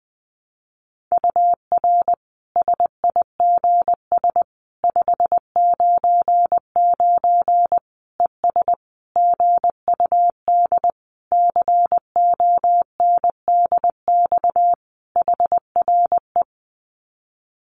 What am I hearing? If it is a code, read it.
Morse code, heard as a keyed tone, about 20 wpm, 700 Hz. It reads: UR SIGS 599 ES GUD CONDX HRE
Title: Common QSO Phrases 20wpm